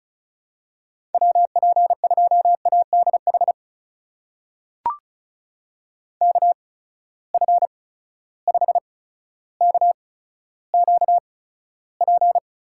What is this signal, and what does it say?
Morse code, heard as a keyed tone, about 36 wpm, spaced as for 15 wpm, 700 Hz. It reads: WP2ADH KF5KQP
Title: USA Call Signs - 6 Characters - 8x Character Spacing - 35wpm